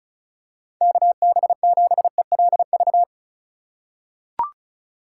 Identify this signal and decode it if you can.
Morse code, heard as a keyed tone, about 36 wpm, 700 Hz. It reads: KB7ELV
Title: USA Call Signs - 6 Characters - 2x Character Spacing - 35wpm